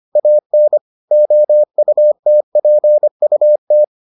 Morse code, heard as a keyed tone, about 25 wpm, 600 Hz. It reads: AN OUTPUT